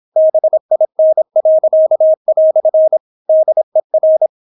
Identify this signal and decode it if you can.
Morse code, heard as a keyed tone, about 26 wpm, 630 Hz. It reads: BIN." DER